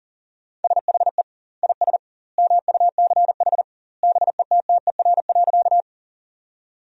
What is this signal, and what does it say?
Morse code, heard as a keyed tone, about 40 wpm, 700 Hz. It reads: SHE IS MUCH BETTER.
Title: 2x Word-Spacing - Sentences from Top 400 Words - 40wpm